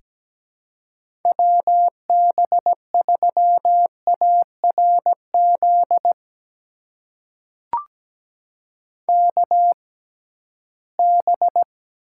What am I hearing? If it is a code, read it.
Morse code, heard as a keyed tone, about 17 wpm, spaced as for 13 wpm, 700 Hz. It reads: WB3ARZ KB
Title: USA Call Signs - 6 Characters - 6x Character Spacing - 17wpm